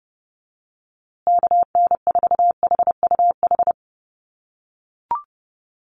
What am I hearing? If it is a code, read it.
Morse code, heard as a keyed tone, about 28 wpm, 700 Hz. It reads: KN4HUH
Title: USA Call Signs Repeated 3x 30wpm